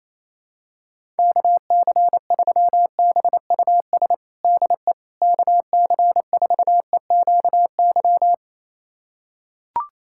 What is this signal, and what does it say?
Morse code, heard as a keyed tone, about 28 wpm, 700 Hz. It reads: KC3BUS DE KC4EQY
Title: USA Callsign DE Callsign 28wpm